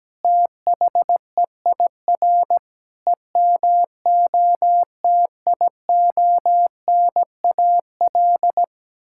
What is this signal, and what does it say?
Morse code, heard as a keyed tone, about 17 wpm, 700 Hz. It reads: THEIR EMOTIONAL
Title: Sentences from Top 1000 Words - Review Part 5 - 17wpm